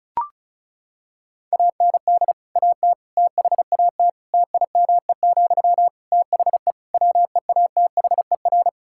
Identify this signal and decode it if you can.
Morse code, heard as a keyed tone, about 36 wpm, 700 Hz. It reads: AND AT THAT TIME, THE WEATHER
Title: Sentences from Top 1000 Words - Review Part 5 - 35wpm